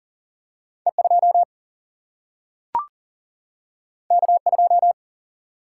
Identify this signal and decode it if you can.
Morse code, heard as a keyed tone, about 40 wpm, 700 Hz. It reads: E2 K2
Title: Single-Letter Single-Number 40wpm